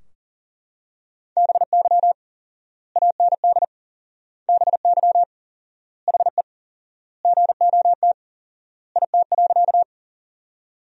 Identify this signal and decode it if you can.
Morse code, heard as a keyed tone, about 40 wpm, 700 Hz. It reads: BY AND BY HE GOT IT.
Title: 4x Word-Spacing - Sentences from Top 300 Words - 40wpm